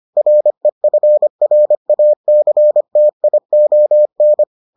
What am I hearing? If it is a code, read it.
Morse code, heard as a keyed tone, about 25 wpm, 600 Hz. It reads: REFRACTION